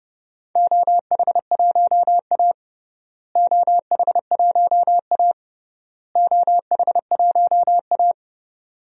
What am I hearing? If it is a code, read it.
Morse code, heard as a keyed tone, about 32 wpm, 700 Hz. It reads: OH1A OH1A OH1A
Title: DXCC-100 Call Signs - 4 Characters - Repeated 3x - 30wpm